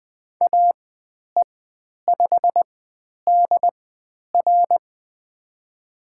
Morse code, heard as a keyed tone, about 20 wpm, spaced as for 10 wpm, 700 Hz. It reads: AE5DR
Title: 102 Lesson 17g - Letter u - Call Signs 20wpm